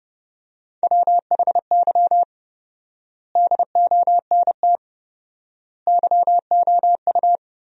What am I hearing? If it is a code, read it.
Morse code, heard as a keyed tone, about 30 wpm, 700 Hz. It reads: WHY DONT YOU
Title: 4x Word-Spacing - Sentences from Top 300 Words - 30wpm